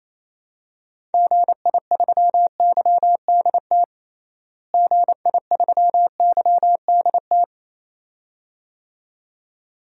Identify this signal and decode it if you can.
Morse code, heard as a keyed tone, about 28 wpm, 700 Hz. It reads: GI3YDT GI3YDT
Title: DXCC-100 Call Signs - 6 Characters - Repeated 2x - 28wpm